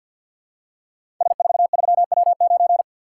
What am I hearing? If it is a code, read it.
Morse code, heard as a keyed tone, about 48 wpm, 700 Hz. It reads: S43W9
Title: Code Groups - 5 Letter:Numbers - 50wpm